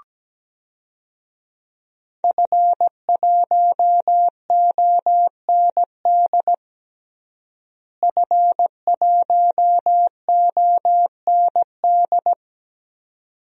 Morse code, heard as a keyed tone, about 17 wpm, 700 Hz. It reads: F1OND F1OND
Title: DXCC-100 Call Signs - 5 Characters - Repeated 2x - 17wpm